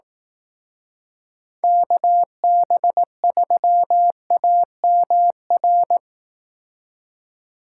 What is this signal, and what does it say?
Morse code, heard as a keyed tone, about 18 wpm, 700 Hz. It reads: KB3AMR
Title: Speed Racing - USA 2x3 Call Signs 15wpm